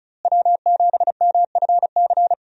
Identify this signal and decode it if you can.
Morse code, heard as a keyed tone, about 34 wpm, 700 Hz. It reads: W7MFC
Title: USA Call Signs - 5 Characters - Repeated 3x - 35wpm